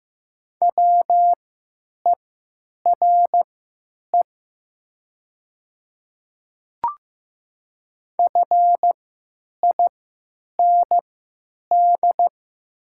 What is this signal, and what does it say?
Morse code, heard as a keyed tone, about 15 wpm, spaced as for 8 wpm, 700 Hz. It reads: WERE FIND
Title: 3x Character Spacing - Top 100 Words - 15wpm